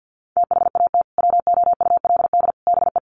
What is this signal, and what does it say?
Morse code, heard as a keyed tone, about 48 wpm, 700 Hz. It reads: THAT WOULD BE
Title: Sets of 3 Words 50wpm